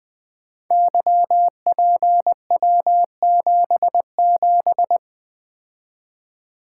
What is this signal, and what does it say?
Morse code, heard as a keyed tone, about 20 wpm, 700 Hz. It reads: YPW77
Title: Code Groups - 5 Letter:Numbers - 20wpm